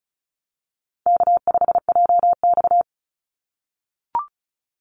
Morse code, heard as a keyed tone, about 34 wpm, 700 Hz. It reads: K5JX
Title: Call Signs - 35wpm